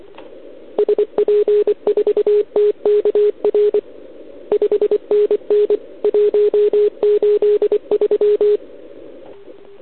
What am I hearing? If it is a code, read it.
Morse code, heard as a keyed tone, about 25 wpm, 410 Hz. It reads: SP4TKR 5NN 183
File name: ARCHIV / OKDX2019 / OKOM40 KX3